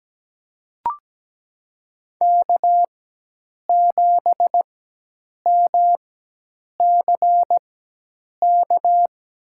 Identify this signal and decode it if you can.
Morse code, heard as a keyed tone, about 17 wpm, spaced as for 8 wpm, 700 Hz. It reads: K7MCK
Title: USA Call Signs - 5 Characters - 4x Character Spacing - 17wpm